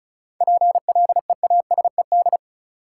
Morse code, heard as a keyed tone, about 34 wpm, 700 Hz. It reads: PLEASED
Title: Seven Letter Words - 35wpm